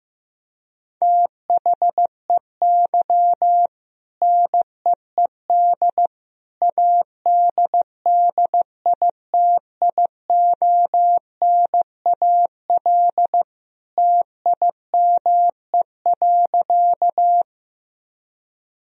Morse code, heard as a keyed tone, about 15 wpm, 700 Hz. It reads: THEY NEED ADDITIONAL TIME.
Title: Sentences from Top 1000 Words - Review Part 4 - 15wpm